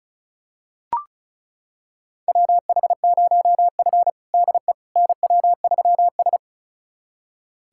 Morse code, heard as a keyed tone, about 36 wpm, 700 Hz. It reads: WH0F DE NW3S
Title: USA Callsign DE Callsign - 4 Characters - 35wpm